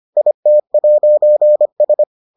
Morse code, heard as a keyed tone, about 25 wpm, 600 Hz. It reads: IT'S